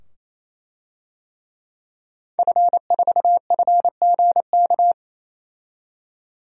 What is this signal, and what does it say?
Morse code, heard as a keyed tone, about 28 wpm, 700 Hz. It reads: F4FGK
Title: DXCC-100 Call Signs - 5 Characters - 28wpm